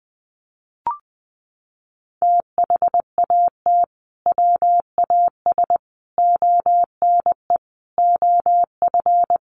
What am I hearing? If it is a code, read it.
Morse code, heard as a keyed tone, about 20 wpm, 700 Hz. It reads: THAT WAS ONE OF
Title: Sets of 5 Words 20wpm